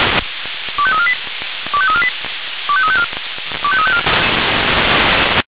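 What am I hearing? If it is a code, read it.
Five-tone selective call (ZVEI), heard as quick runs of five tones, tones of about 70 ms. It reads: 25348 25348 25354 25354
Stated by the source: ZVEI five-tone selective call received by SDR.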